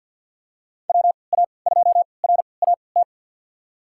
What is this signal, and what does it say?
Morse code, heard as a keyed tone, about 50 wpm, spaced as for 28 wpm, 700 Hz. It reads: WA2RAT
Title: USA Call Signs - 6 Characters - 3x Character Spacing - 50wpm